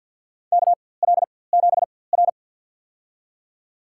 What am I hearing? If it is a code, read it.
Morse code, heard as a keyed tone, about 50 wpm, spaced as for 23 wpm, 700 Hz. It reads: KL7R
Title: USA Call Signs - 4x Character Spacing - 50wpm